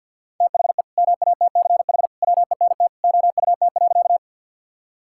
Morse code, heard as a keyed tone, about 51 wpm, 700 Hz. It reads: THE MATCH WENT OUT.
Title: Sentences from Top 1600 Words 50wpm